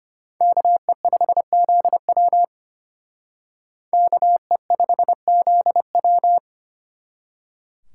Morse code, heard as a keyed tone, about 33 wpm, 700 Hz. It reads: KE5ZW KE5ZW